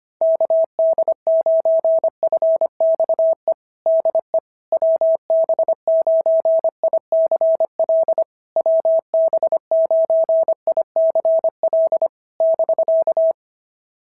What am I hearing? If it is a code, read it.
Morse code, heard as a keyed tone, about 25 wpm, 650 Hz. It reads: KD9FXE DE WB9ICL WB9ICL <BK>
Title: SOTA QSO Practice 25wpm